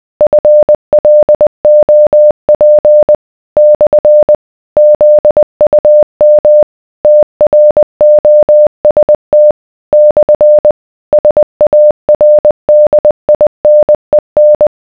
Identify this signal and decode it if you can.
Morse code, heard as a keyed tone, about 20 wpm, 600 Hz. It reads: FLOP / ZUM TROST / SARDINEN